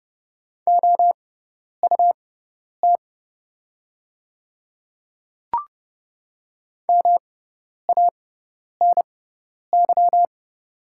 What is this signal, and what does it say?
Morse code, heard as a keyed tone, about 31 wpm, spaced as for 10 wpm, 700 Hz. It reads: OUT MANY
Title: 6x Character Spacing - Top 100 Words - 30wpm